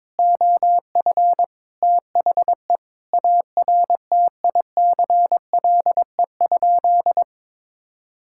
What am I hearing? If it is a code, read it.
Morse code, heard as a keyed tone, about 22 wpm, 700 Hz. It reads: OF THE ARTICLE?
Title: Sentences from Top 1000 Words - Review Part 5 - 22wpm